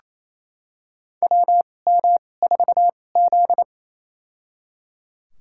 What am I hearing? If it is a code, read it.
Morse code, heard as a keyed tone, about 28 wpm, spaced as for 20 wpm, 700 Hz. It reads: WM4Z
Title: USA Call Signs - 4 Characters - 2x Character Spacing - 28wpm